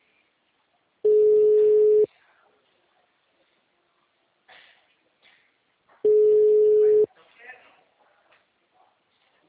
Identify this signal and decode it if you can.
Ringback tone (Europe, 425 Hz), with speech.